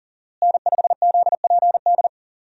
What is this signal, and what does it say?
Morse code, heard as a keyed tone, about 40 wpm, 700 Hz. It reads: N5ZPD
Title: USA Call Signs - 5 Characters - Repeated 2x - 40wpm